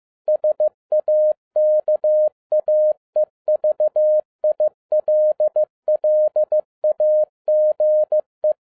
Morse code, heard as a keyed tone, about 15 wpm, 610 Hz. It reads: SAKAE VILLAGE
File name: 2012-01-mp3 / n13-A NEWLY DESIGNED QSL CARD-15wpm